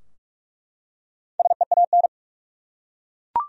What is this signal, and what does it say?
Morse code, heard as a keyed tone, about 48 wpm, 700 Hz. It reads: SEAN
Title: USA Common Names - Max 5 Letters - 45wpm